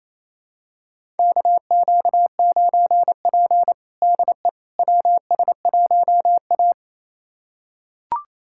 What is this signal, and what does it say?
Morse code, heard as a keyed tone, about 28 wpm, 700 Hz. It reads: KQ9P DE WS1A
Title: USA Callsign DE Callsign - 4 Characters - 28wpm